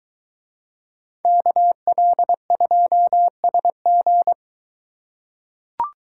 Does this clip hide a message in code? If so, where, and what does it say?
Yes: Morse code from 1.2 to 4.3 s, heard as a keyed tone, about 23 wpm, 700 Hz. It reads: KL2SG